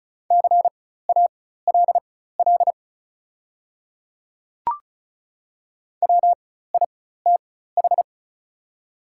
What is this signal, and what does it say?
Morse code, heard as a keyed tone, about 36 wpm, spaced as for 16 wpm, 700 Hz. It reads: CALL WITH
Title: 4x Character Spacing - Top 100 Words - 35wpm